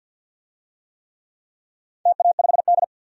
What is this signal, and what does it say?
Morse code, heard as a keyed tone, about 51 wpm, 700 Hz. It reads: TA5D
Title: DXCC-100 Call Signs - 4 Characters - 50wpm